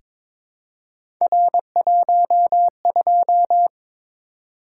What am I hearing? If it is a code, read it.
Morse code, heard as a keyed tone, about 23 wpm, 700 Hz. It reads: R12
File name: practice / 22wpm / Three Letter-Number Combinations - 22wpm